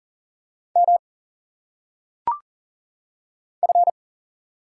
Morse code, heard as a keyed tone, about 40 wpm, 700 Hz. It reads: M F
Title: Single Letters - Phonetic Pronunciation 40wpm